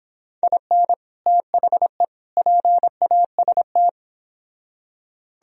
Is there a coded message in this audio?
Yes: Morse code from 0.4 to 3.9 s, heard as a keyed tone, about 31 wpm, 700 Hz. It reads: IN THE PAST